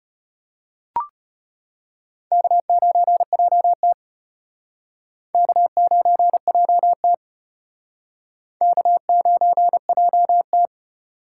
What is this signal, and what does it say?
Morse code, heard as a keyed tone, about 31 wpm, 700 Hz. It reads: K9JT K9JT K9JT
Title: Speed Racing - USA Call Signs 25wpm